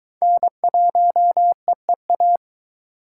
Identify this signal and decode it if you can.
Morse code, heard as a keyed tone, about 20 wpm, 700 Hz. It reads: N1EEA